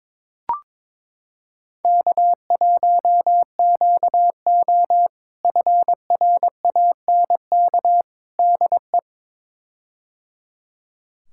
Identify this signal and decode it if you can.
Morse code, heard as a keyed tone, about 22 wpm, 700 Hz. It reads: K1QO FRANK DE
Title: K1USN SST QSO Practice - 22wpm